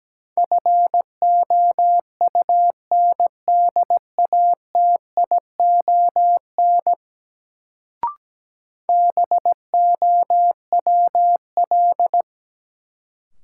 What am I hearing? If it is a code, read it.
Morse code, heard as a keyed tone, about 17 wpm, 700 Hz. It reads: FOUNDATION BOWL
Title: Sentences from Top 1200 Words - New Word Drill - 17wpm